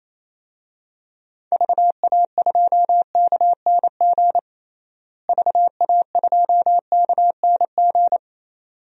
Morse code, heard as a keyed tone, about 27 wpm, 700 Hz. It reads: VA2KNG VA2KNG
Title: DXCC-100 Call Signs - 6 Characters - Repeated 2x - 28wpm